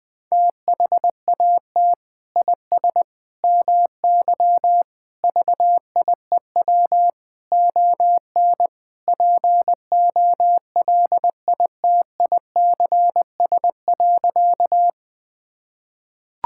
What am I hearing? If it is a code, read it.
Morse code, heard as a keyed tone, about 20 wpm, 700 Hz. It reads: THAT IS MY VIEW ON POLITICS.
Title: Sentences from Top 1000 Words - Review Part 6 - 20wpm